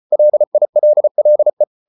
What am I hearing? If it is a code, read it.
Morse code, heard as a keyed tone, about 34 wpm, 600 Hz. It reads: LILLE
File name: Ferie QRQ 34wpm